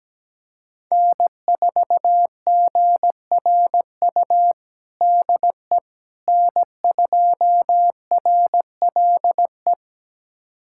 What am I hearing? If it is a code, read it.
Morse code, heard as a keyed tone, about 17 wpm, 700 Hz. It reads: N4GRU DE N2RLE
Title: USA Callsign DE Callsign - 5 Characters - 17wpm